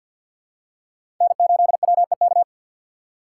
Morse code, heard as a keyed tone, about 53 wpm, 700 Hz. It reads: N8WEK
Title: USA Call Signs - 5 Characters - 2x Character Spacing - 50wpm